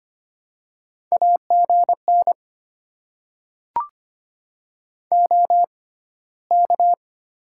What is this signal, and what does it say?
Morse code, heard as a keyed tone, about 25 wpm, spaced as for 15 wpm, 700 Hz. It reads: AGN OK
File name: Top 90 QSO Elements - 6x Character Spacing - 25wpm